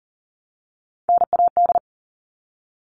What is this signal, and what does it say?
Morse code, heard as a keyed tone, about 40 wpm, 700 Hz. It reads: NAD
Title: All Cut Numbers - 3-Digit 40wpm